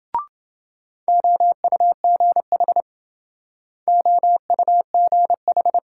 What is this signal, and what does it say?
Morse code, heard as a keyed tone, about 26 wpm, 700 Hz. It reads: OUGH OUGH